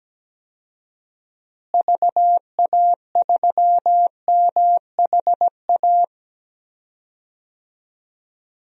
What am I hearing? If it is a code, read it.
Morse code, heard as a keyed tone, about 17 wpm, 700 Hz. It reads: VA3MHA
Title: DXCC-100 Call Signs - 6 Characters - Repeated 2x - 17wpm